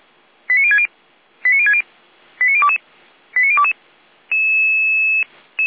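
Five-tone selective call (ZVEI), heard as quick runs of five tones, tones of about 70 ms. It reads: 89060 89060 89022 89022
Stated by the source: Received by SDR; fire brigade (Feuerwehr) Orlamünde.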